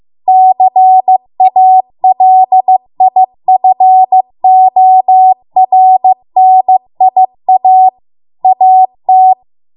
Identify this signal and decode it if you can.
Morse code, heard as a keyed tone, about 15 wpm, 750 Hz. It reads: CALIFORNIA AT